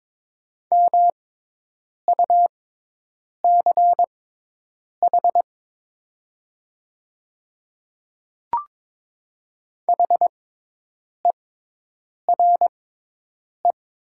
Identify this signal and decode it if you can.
Morse code, heard as a keyed tone, about 22 wpm, spaced as for 8 wpm, 700 Hz. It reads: MUCH HERE
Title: 6x Character Spacing - Top 300 Words - 22wpm